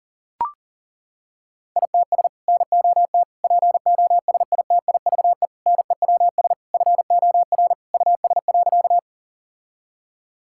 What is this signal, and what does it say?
Morse code, heard as a keyed tone, about 40 wpm, 700 Hz. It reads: ITS NOT POSITIVE NEWS FOR US.
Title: Sentences from Top 1000 Words - Review Part 5 - 40wpm